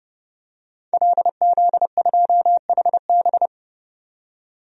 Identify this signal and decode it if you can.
Morse code, heard as a keyed tone, about 31 wpm, 700 Hz. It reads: LZ2HB
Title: DXCC-100 Call Signs - 5 Characters - 30wpm